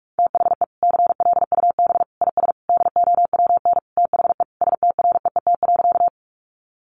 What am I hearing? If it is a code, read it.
Morse code, heard as a keyed tone, about 44 wpm, 700 Hz. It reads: THE CLUB IS DOWN THE STREET.
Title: Sentences from Top 1200 Words 45wpm